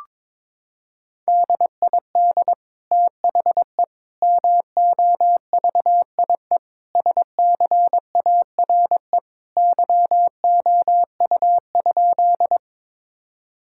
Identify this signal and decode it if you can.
Morse code, heard as a keyed tone, about 21 wpm, 700 Hz. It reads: DID THE MOVIE SCARE YOU?
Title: Sentences from Top 1000 Words - Review Part 5 - 22wpm